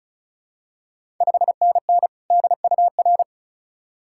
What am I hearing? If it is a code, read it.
Morse code, heard as a keyed tone, about 36 wpm, 700 Hz. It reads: 5NN DUR
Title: Ontario QSO Party Exchange 35wpm